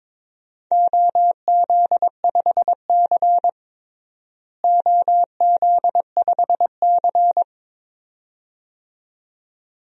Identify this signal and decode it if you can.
Morse code, heard as a keyed tone, about 22 wpm, 700 Hz. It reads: OZ5C OZ5C
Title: DXCC-100 Call Signs - 4 Characters - Repeated 2x - 22wpm